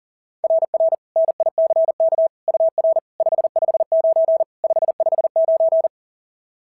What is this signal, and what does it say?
Morse code, heard as a keyed tone, about 40 wpm, 660 Hz. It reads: RR NICK UR 559 559
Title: SOTA QSO Practice 40wpm